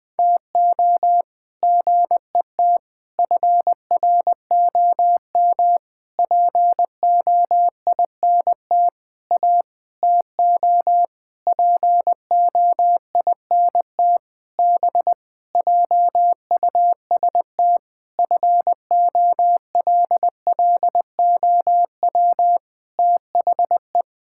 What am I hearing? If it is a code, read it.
Morse code, heard as a keyed tone, about 20 wpm, 700 Hz. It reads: TO GET FROM POINT A TO POINT B JUST FOLLOW THE
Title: Sentences with 8-Word Phrases - 20wpm